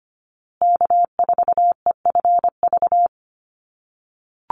Morse code, heard as a keyed tone, about 25 wpm, 700 Hz. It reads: K4EFV
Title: USA Call Signs Repeated 3x 25wpm